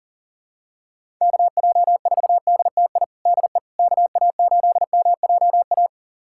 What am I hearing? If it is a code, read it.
Morse code, heard as a keyed tone, about 40 wpm, 700 Hz. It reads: KJ4DTI DE KA8MJA
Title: USA Callsign DE Callsign 40wpm